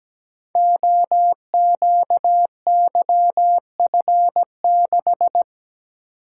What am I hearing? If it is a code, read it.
Morse code, heard as a keyed tone, about 17 wpm, 700 Hz. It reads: OQYF6
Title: Code Groups - 5 Letter:Numbers - 17wpm